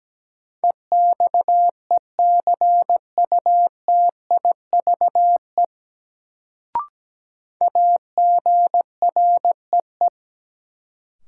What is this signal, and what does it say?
Morse code, heard as a keyed tone, about 17 wpm, 700 Hz. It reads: EXECUTIVE AGREE
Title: Sets of 5 Words Encore - New Word Drill - 17wpm